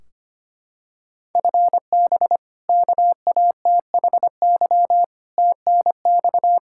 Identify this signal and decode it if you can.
Morse code, heard as a keyed tone, about 25 wpm, 700 Hz. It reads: FB KATHY TNX
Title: Practice QSO - Exchange 3 - More Info - 25wpm